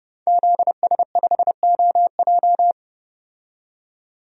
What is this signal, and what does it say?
Morse code, heard as a keyed tone, about 31 wpm, 700 Hz. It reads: ZS5OJ
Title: DXCC-100 Call Signs - 5 Characters - 30wpm